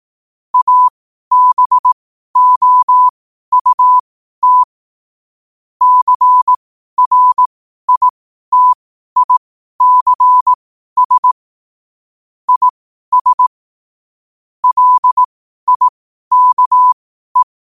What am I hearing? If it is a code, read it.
Morse code, heard as a keyed tone, about 18 wpm, spaced as for 12 wpm, 1000 Hz. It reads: ABOUT CRITICS IS LIKE
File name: qotdfiles / 12wpm